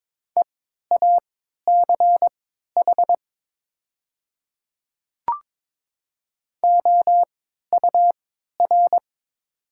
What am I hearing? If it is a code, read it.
Morse code, heard as a keyed tone, about 22 wpm, spaced as for 12 wpm, 690 Hz. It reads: EACH OUR
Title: 3x Character Spacing - Top 200 Words - 22wpm